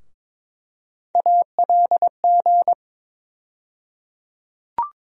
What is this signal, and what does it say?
Morse code, heard as a keyed tone, about 22 wpm, 700 Hz. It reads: ALG
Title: USA Call Sign Postfixes - 3 Letters 22wpm